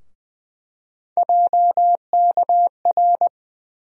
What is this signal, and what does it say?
Morse code, heard as a keyed tone, about 20 wpm, 700 Hz. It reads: JKR
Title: USA Call Sign Postfixes - 3 Letters 20wpm